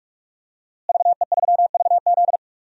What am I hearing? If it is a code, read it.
Morse code, heard as a keyed tone, about 41 wpm, 700 Hz. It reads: VE3VZ
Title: DXCC-100 Call Signs - 5 Characters - Repeated 2x - 45wpm